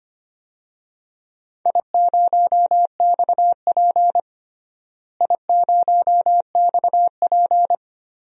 Morse code, heard as a keyed tone, about 25 wpm, 700 Hz. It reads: I0XP I0XP
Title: DXCC-100 Call Signs - 4 Characters - Repeated 2x - 25wpm